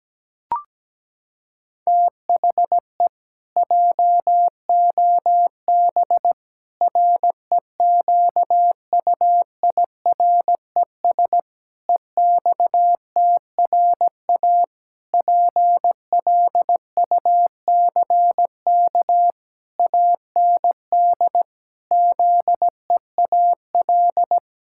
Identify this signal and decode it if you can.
Morse code, heard as a keyed tone, about 17 wpm, 700 Hz. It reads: THE JOB REQUIRES EXTRA PLUCK AND ZEAL
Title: Pangram Challenge v2 - 17wpm